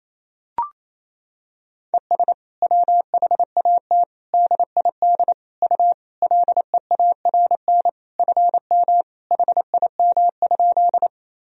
Morse code, heard as a keyed tone, about 28 wpm, 700 Hz. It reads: ES WHAT DID U LEARN FM HIM?
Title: Sentences from Top 300 Words and Abbreviations 28wpm